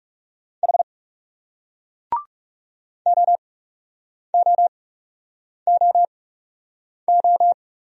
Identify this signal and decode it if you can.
Morse code, heard as a keyed tone, about 40 wpm, spaced as for 8 wpm, 700 Hz. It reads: H OOOO